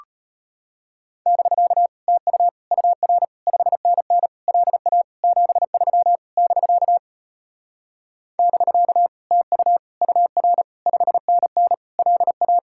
Morse code, heard as a keyed tone, about 33 wpm, 700 Hz. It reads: <BK> TU UR 5NN LA 73 <BK> <BK> TU UR 5NN LA